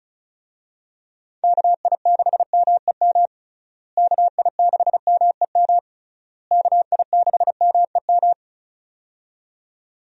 Morse code, heard as a keyed tone, about 36 wpm, 700 Hz. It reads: KI6MEM KI6MEM KI6MEM
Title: USA Call Signs - 6 Characters - Repeated 3x - 35wpm